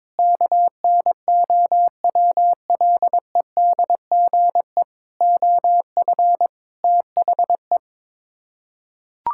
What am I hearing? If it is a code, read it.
Morse code, heard as a keyed tone, about 22 wpm, 700 Hz. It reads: KNOWLEDGE OF THE
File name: practice / 22wpm / Sets of 8 Words - 22wpm